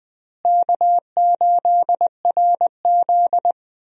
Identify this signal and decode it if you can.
Morse code, heard as a keyed tone, about 21 wpm, 700 Hz. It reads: K8RZ
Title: USA Call Signs - 4 Characters - Repeated 2x - 20wpm